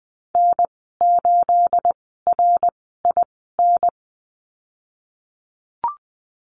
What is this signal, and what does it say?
Morse code, heard as a keyed tone, about 20 wpm, spaced as for 15 wpm, 700 Hz. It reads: N8RIN
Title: 231 Lesson 37g - Letter x - Call Signs 20wpm